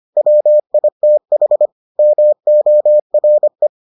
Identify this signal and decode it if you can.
Morse code, heard as a keyed tone, about 25 wpm, 600 Hz. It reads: WITH MORE